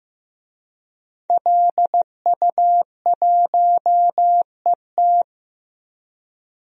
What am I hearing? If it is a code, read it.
Morse code, heard as a keyed tone, about 15 wpm, 700 Hz. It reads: LU1ET
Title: DXCC-100 Call Signs - 5 Characters - 15wpm